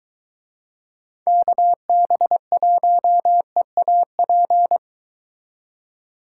Morse code, heard as a keyed tone, about 22 wpm, 700 Hz. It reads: KB1EAP